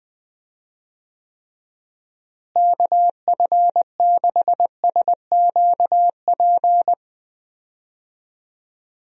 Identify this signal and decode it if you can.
Morse code, heard as a keyed tone, about 20 wpm, 700 Hz. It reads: KF6SQP
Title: USA Call Signs - 6 Characters - Repeated 2x - 20wpm